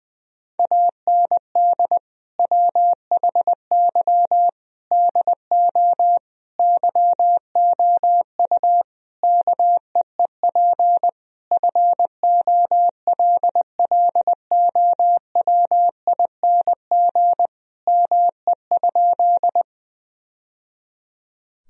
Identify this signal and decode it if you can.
Morse code, heard as a keyed tone, about 20 wpm, 700 Hz. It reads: AND WHY DO YOU KEEP FOLLOWING ME?
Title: Sentences from Top 1000 Words - Review Part 4 - 20wpm